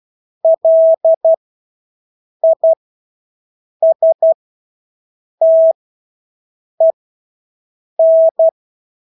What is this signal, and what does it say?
Morse code, heard as a keyed tone, about 12 wpm, spaced as for 6 wpm, 650 Hz. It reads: LISTEN
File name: Engelske ord 12-6 wpm